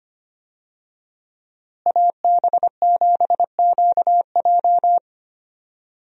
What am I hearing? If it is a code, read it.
Morse code, heard as a keyed tone, about 25 wpm, 700 Hz. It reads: AB7QJ
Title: USA Call Signs - 5 Characters - Repeated 2x - 25wpm